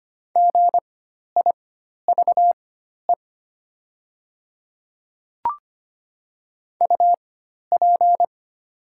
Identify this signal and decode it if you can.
Morse code, heard as a keyed tone, about 25 wpm, spaced as for 12 wpm, 700 Hz. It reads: GIVE UP
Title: 4x Character Spacing - Top 200 Words - 25wpm